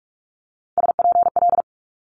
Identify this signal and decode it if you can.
Morse code, heard as a keyed tone, about 46 wpm, 700 Hz. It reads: SPL
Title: Multi-letter Phonemes Quick Fire 45wpm